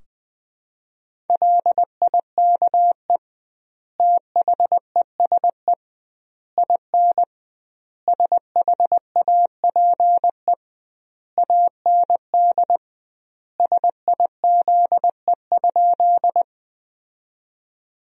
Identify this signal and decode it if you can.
Morse code, heard as a keyed tone, about 21 wpm, 700 Hz. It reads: LIKE THESE IN SHAPE AND SIZE?
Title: 2x Word-Spacing - Sentences from Top 500 Words - 20wpm